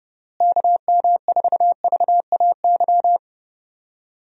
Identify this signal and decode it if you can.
Morse code, heard as a keyed tone, about 30 wpm, 700 Hz. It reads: KM4VAY
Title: USA Call Signs - 6 Characters - Repeated 2x - 30wpm